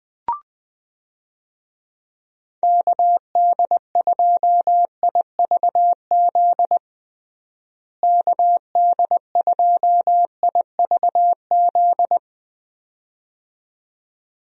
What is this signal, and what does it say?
Morse code, heard as a keyed tone, about 20 wpm, 700 Hz. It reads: KD2IVZ KD2IVZ
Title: USA Call Signs - 6 Characters - Repeated 2x - 20wpm